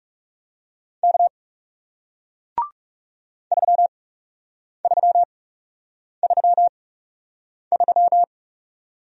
Morse code, heard as a keyed tone, about 40 wpm, spaced as for 8 wpm, 700 Hz. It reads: K 3333